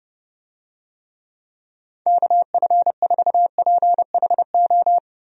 Morse code, heard as a keyed tone, about 30 wpm, 700 Hz. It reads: KF4PHO
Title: USA Call Signs - 6 Characters - Repeated 3x - 30wpm